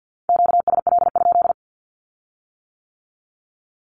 Morse code, heard as a keyed tone, about 50 wpm, 700 Hz. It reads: QSL?
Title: Q-Codes with Spoken Meaning - 50wpm